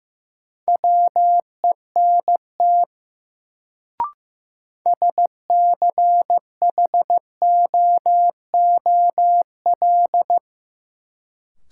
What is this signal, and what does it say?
Morse code, heard as a keyed tone, about 15 wpm, 700 Hz. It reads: WENT SCHOOL
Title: Sets of 2 Words - New Word Drill - 15wpm